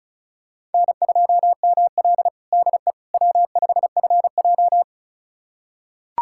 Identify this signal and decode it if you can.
Morse code, heard as a keyed tone, about 36 wpm, 700 Hz. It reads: N2ML DE W5FJ
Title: USA Callsign DE Callsign - 4 Characters - 35wpm